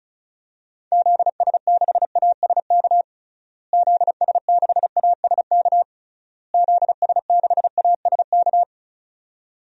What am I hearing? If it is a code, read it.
Morse code, heard as a keyed tone, about 36 wpm, 700 Hz. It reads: ZS6ASK ZS6ASK ZS6ASK
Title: DXCC-100 Call Signs - 6 Characters - Repeated 3x - 35wpm